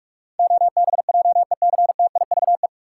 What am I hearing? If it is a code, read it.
Morse code, heard as a keyed tone, about 46 wpm, 700 Hz. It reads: OBJECTIVE
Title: Sentences from Top 1200 Words - New Word Drill - 45wpm